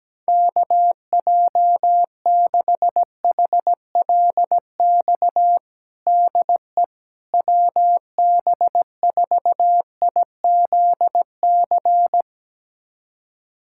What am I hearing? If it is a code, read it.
Morse code, heard as a keyed tone, about 17 wpm, 700 Hz. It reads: KJ6HLX DE WB4IZC
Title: USA Callsign DE Callsign - 6 Characters - 17wpm